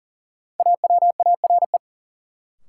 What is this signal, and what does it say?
Morse code, heard as a keyed tone, about 40 wpm, 700 Hz. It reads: AWARE
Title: Sentences from Top 1200 Words - New Word Drill - 40wpm